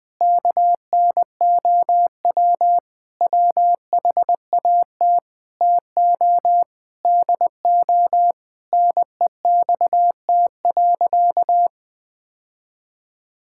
Morse code, heard as a keyed tone, about 20 wpm, 700 Hz. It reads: KNOW WHAT TO DO NEXT.
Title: Sentences with 5-Word Phrases - Spoken Beginning - 20wpm